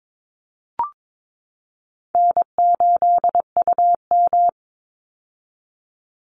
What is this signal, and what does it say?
Morse code, heard as a keyed tone, about 22 wpm, 700 Hz. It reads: N8UM
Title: Easy Call Signs - 22wpm